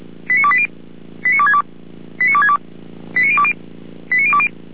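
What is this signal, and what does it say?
Five-tone selective call (ZVEI), heard as quick runs of five tones, tones of about 70 ms. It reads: 89280 89262 89262 89020 89020
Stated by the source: ZVEI five-tone selective call 89333 received by SDR.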